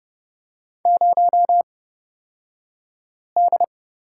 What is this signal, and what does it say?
Morse code, heard as a keyed tone, about 30 wpm, spaced as for 5 wpm, 700 Hz. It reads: 0D
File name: Lesson 37d - Letter x - Two Characters 30wpm